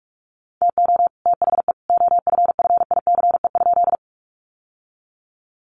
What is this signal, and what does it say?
Morse code, heard as a keyed tone, about 44 wpm, 700 Hz. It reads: TO THE OFFICE?
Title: Sentences from Top 1000 Words 45wpm